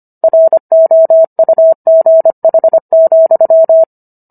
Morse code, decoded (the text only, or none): ROUGH,